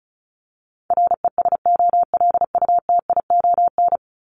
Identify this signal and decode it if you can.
Morse code, heard as a keyed tone, about 34 wpm, 700 Hz. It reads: RESOLUTION
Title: Top 1501-1750 Words 35wpm